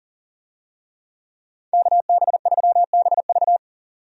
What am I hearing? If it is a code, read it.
Morse code, heard as a keyed tone, about 40 wpm, 700 Hz. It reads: KB3BV
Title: USA Call Signs - 5 Characters - 40wpm